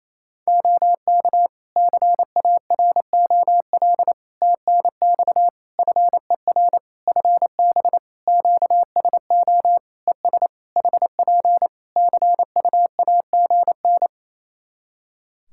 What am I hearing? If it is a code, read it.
Morse code, heard as a keyed tone, about 28 wpm, 700 Hz. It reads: OK CAROL TNX FER FB QSO ES HP CUAGN
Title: Practice QSO - Exchange 4 - Ending - 28wpm